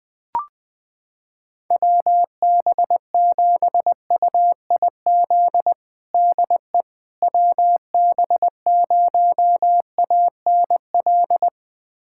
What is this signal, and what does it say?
Morse code, heard as a keyed tone, about 21 wpm, 700 Hz. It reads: WB7UIZ DE WB0ANL
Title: USA Callsign DE Callsign - 6 Characters - 20wpm